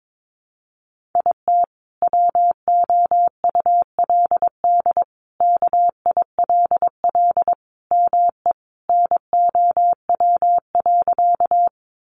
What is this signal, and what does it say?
Morse code, heard as a keyed tone, about 22 wpm, 700 Hz. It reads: IT WOULD KILL ME NOW.
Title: Sentences from Top 1400 Words 22wpm